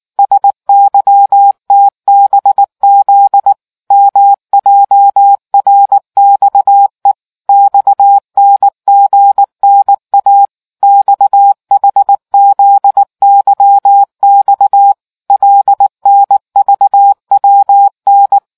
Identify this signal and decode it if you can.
Morse code, heard as a keyed tone, about 19 wpm, 800 Hz. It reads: SYTBZ MJRXE XNGNA XHZYX LNVWN